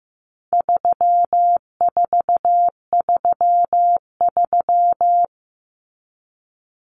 Morse code, heard as a keyed tone, about 15 wpm, 700 Hz. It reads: 3433
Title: Numbers - 4-Digit 15wpm